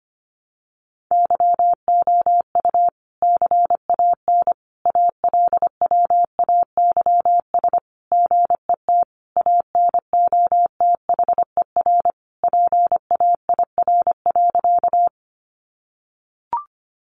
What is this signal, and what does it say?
Morse code, heard as a keyed tone, about 25 wpm, 700 Hz. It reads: YOU CAN ALWAYS GET ANOTHER PAIR.
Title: Sentences from Top 1200 Words 25wpm